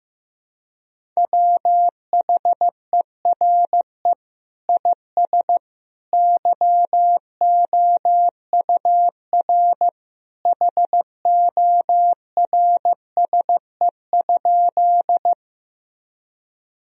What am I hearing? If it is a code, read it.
Morse code, heard as a keyed tone, about 15 wpm, 700 Hz. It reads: WHERE IS YOUR HORSE?
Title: Sentences from Top 1000 Words - Review Part 1 - 15wpm